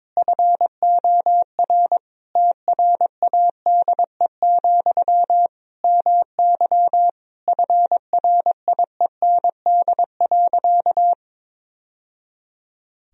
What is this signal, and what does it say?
Morse code, heard as a keyed tone, about 22 wpm, 700 Hz. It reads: FOR TRADE, MY FRIEND.
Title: Sentences from Top 1000 Words - Review Part 4 - 22wpm